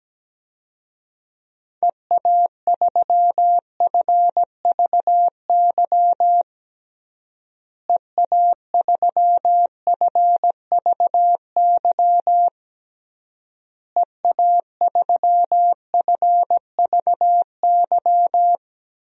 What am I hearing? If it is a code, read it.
Morse code, heard as a keyed tone, about 17 wpm, 700 Hz. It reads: EA3FVY EA3FVY EA3FVY
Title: DXCC-100 Call Signs - 6 Characters - Repeated 3x - 17wpm